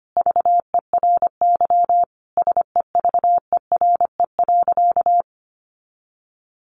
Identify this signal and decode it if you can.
Morse code, heard as a keyed tone, about 25 wpm, 700 Hz. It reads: VERY SEVERE.
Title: Sentences from Top 1400 Words 25wpm